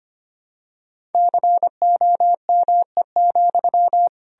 Morse code, heard as a keyed tone, about 25 wpm, 700 Hz. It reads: COME,
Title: Sentences from Top 1000 Words - Review Part 2 - 25wpm